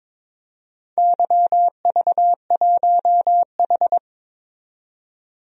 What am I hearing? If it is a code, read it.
Morse code, heard as a keyed tone, about 22 wpm, 700 Hz. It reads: YV1H
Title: DXCC-100 Call Signs - 4 Characters - 22wpm